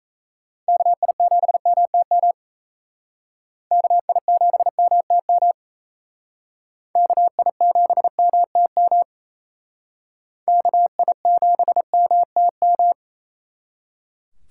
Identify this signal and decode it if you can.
Morse code, heard as a keyed tone, about 36 wpm, 700 Hz. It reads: KI7MTM KI7MTM KI7MTM KI7MTM